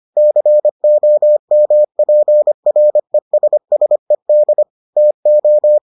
Morse code, heard as a keyed tone, about 25 wpm, 600 Hz. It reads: COMPRESSED TO